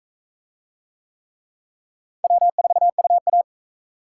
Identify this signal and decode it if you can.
Morse code, heard as a keyed tone, about 40 wpm, 700 Hz. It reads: W4UA